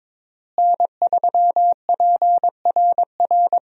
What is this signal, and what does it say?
Morse code, heard as a keyed tone, about 22 wpm, 700 Hz. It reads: N3PRR
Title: USA Call Signs - 5 Characters - Repeated 3x - 22wpm